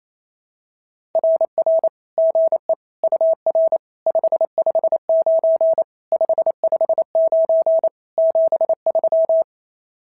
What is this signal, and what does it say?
Morse code, heard as a keyed tone, about 28 wpm, 650 Hz. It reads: RR GE UR 559 559 73
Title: SOTA QSO Practice 28wpm